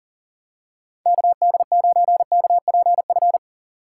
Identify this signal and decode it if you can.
Morse code, heard as a keyed tone, about 40 wpm, 700 Hz. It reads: KD9KPF
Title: USA Call Signs - 6 Characters - 3x Character Spacing - 40wpm